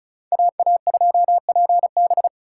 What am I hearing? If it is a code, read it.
Morse code, heard as a keyed tone, about 45 wpm, 700 Hz. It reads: AA2PB